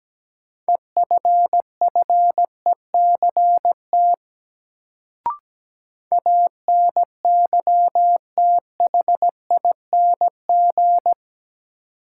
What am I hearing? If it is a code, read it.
Morse code, heard as a keyed tone, about 17 wpm, 700 Hz. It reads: EFFECT ANYTHING
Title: Sentences from Top 1000 Words - New Word Drill - 17wpm